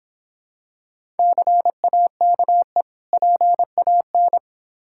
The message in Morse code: CAKE PAN

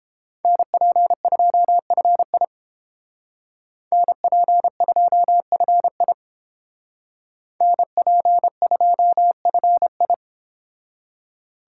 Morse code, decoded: NP2FI NP2FI NP2FI

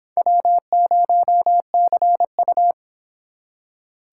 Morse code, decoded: W0CU

